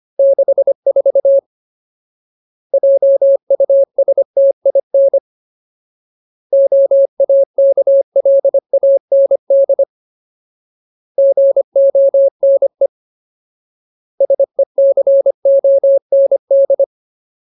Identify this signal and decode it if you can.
Morse code, heard as a keyed tone, about 25 wpm, 550 Hz. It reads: 64 JUSTIN OAKLAND GONE SECOND
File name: Ultimate Word List / 25WPM Ultimate